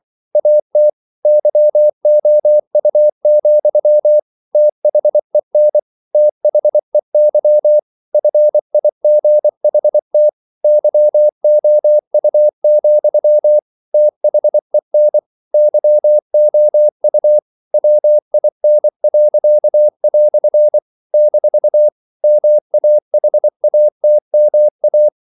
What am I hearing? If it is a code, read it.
Morse code, heard as a keyed tone, about 24 wpm, 600 Hz. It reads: AT YOU, THEN THEY FIGHT YOU, THEN YOU WIN." - MAHATMA
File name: CW / Practice / EnglishQuotations0224